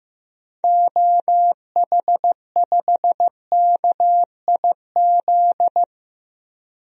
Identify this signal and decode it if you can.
Morse code, heard as a keyed tone, about 15 wpm, 700 Hz. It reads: OH5KIZ
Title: DXCC-100 Call Signs - 6 Characters - Repeated 3x - 15wpm